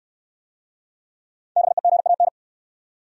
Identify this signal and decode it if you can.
Morse code, heard as a keyed tone, about 68 wpm, 700 Hz. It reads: BEGAN